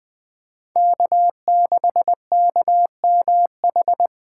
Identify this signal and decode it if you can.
Morse code, heard as a keyed tone, about 20 wpm, 700 Hz. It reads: K6KMH
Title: 2023 SOTA Chaser Call Signs - 20wpm